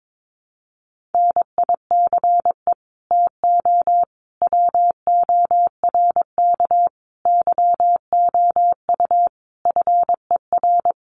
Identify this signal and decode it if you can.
Morse code, heard as a keyed tone, about 22 wpm, 700 Hz. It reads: NICE TO WORK YOU FER
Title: Common QSO Phrases 22wpm